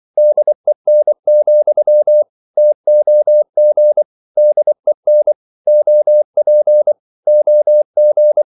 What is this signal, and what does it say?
Morse code, heard as a keyed tone, about 24 wpm, 600 Hz. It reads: DEN, TOG DEN OP OG
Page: Ferie 24 wpm